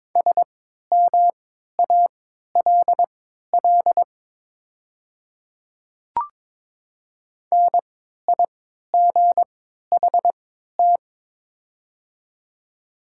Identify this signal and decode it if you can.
Morse code, heard as a keyed tone, about 22 wpm, spaced as for 13 wpm, 690 Hz. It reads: SMALL NIGHT
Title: 3x Character Spacing - Top 300 Words - 22wpm